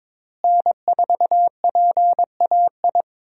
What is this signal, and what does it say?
Morse code, heard as a keyed tone, about 22 wpm, 700 Hz. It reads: N4PAI
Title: USA Call Signs - 5 Characters - Repeated 2x - 22wpm